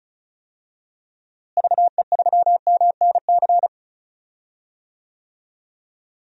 Morse code, heard as a keyed tone, about 34 wpm, 700 Hz. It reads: VE3MNC
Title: DXCC-100 Call Signs - 6 Characters - Repeated 3x - 35wpm